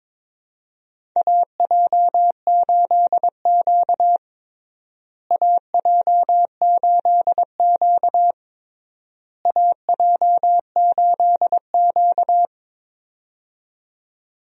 Morse code, heard as a keyed tone, about 22 wpm, 700 Hz. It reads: AJ8Q AJ8Q AJ8Q
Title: USA Call Signs - 4 Characters - Repeated 3x - 22wpm